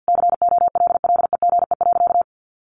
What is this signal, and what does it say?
Morse code, heard as a keyed tone, about 50 wpm, 700 Hz. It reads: COLLEGE.
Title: Sentences from Top 1300 Words 50wpm